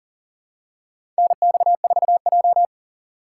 Morse code, heard as a keyed tone, about 38 wpm, 700 Hz. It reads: NX4J
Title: USA Call Signs - 4 Characters - 4x Character Spacing - 40wpm